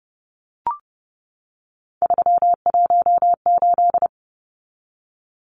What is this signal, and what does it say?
Morse code, heard as a keyed tone, about 30 wpm, 700 Hz. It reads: 318
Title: Numbers - 3-Digit 30wpm